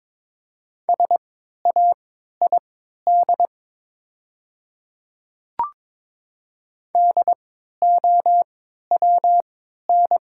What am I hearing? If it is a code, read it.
Morse code, heard as a keyed tone, about 22 wpm, spaced as for 12 wpm, 700 Hz. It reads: SAID DOWN
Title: 3x Character Spacing - Top 100 Words - 22wpm